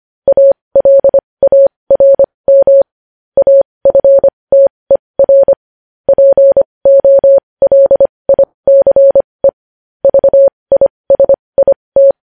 Morse code, heard as a keyed tone, about 25 wpm, spaced as for 20 wpm, 550 Hz. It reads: ALARM AFTER POLICE VISIT
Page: news headlines in morse code at 20 wpm